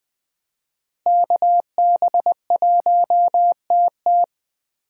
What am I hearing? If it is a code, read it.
Morse code, heard as a keyed tone, about 20 wpm, 700 Hz. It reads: KB1TT
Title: USA Call Signs - 5 Characters - Repeated 2x - 20wpm